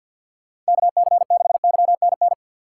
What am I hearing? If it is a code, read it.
Morse code, heard as a keyed tone, about 52 wpm, 700 Hz. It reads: KC6YNN
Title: USA Call Signs - 6 Characters - 6x Character Spacing - 50wpm